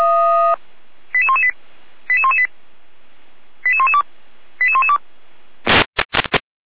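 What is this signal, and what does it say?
Five-tone selective call (ZVEI), heard as quick runs of five tones, tones of about 70 ms. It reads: 88118 88118 88112 88112
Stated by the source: ZVEI five-tone selective call 88441 received by SDR, logged on the Feuerwehr Orlamünde (fire brigade) audio server.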